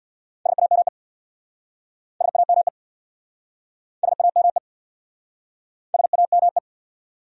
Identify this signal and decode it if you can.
Morse code, heard as a keyed tone, about 66 wpm, 700 Hz. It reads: SAME SAME SAME SAME